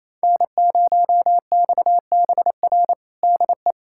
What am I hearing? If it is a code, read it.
Morse code, heard as a keyed tone, about 28 wpm, 700 Hz. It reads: N0XBR DE
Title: USA Callsign DE Callsign - 5 Characters - 28wpm